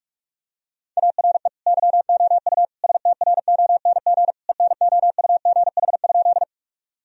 Morse code, heard as a keyed tone, about 42 wpm, 700 Hz. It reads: ARE YOU STRONG ENOUGH?
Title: Sentences from Top 1000 Words - Review Part 3 - 45wpm